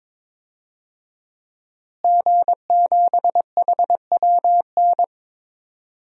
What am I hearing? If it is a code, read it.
Morse code, heard as a keyed tone, about 21 wpm, 700 Hz. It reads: G7HWN
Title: DXCC-100 Call Signs - 5 Characters - Repeated 2x - 22wpm